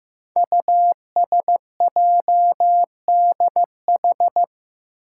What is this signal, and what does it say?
Morse code, heard as a keyed tone, about 15 wpm, 700 Hz. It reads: USJDH
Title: Code Groups - 5 Letters - 15wpm